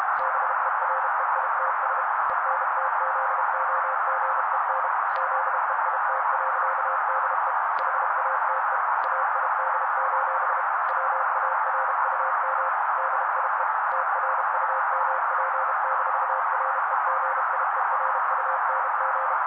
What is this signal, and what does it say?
Morse code, heard as a keyed tone, about 32 wpm, 530 Hz. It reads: BEREITS ENTZOGEN ZEIT.DE HATE AID: WARUM DIE TRUMP-REGIERUNG
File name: daily / 31wpm noise